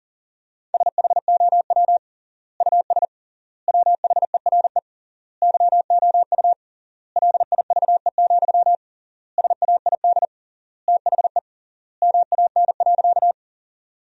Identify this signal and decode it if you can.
Morse code, heard as a keyed tone, about 40 wpm, 700 Hz. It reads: SHOW US WHERE YOU LIVE, SAID THE MAN.
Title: 3x Word-Spacing - Sentences from Top 200 Words - 40wpm